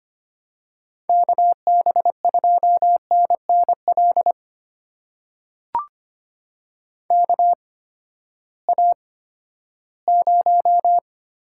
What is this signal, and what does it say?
Morse code, heard as a keyed tone, about 25 wpm, spaced as for 14 wpm, 700 Hz. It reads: KB2NNL KA0
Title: USA Call Signs - 6 Characters - 8x Character Spacing - 25wpm